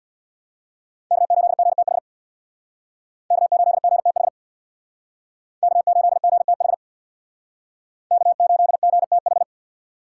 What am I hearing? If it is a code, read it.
Morse code, heard as a keyed tone, about 62 wpm, 700 Hz. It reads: K8GTH K8GTH K8GTH K8GTH
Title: Speed Racing - USA 1x3 Call Signs 50wpm